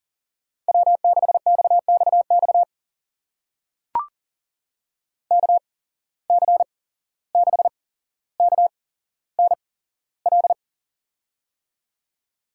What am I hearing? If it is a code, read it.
Morse code, heard as a keyed tone, about 39 wpm, spaced as for 16 wpm, 700 Hz. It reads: W6XXX KC6KNL
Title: USA Call Signs - 8x Character Spacing - 40wpm